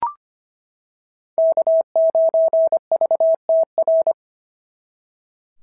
Morse code, heard as a keyed tone, about 25 wpm, 650 Hz. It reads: K9VTR
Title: SOTA QSO Practice 25wpm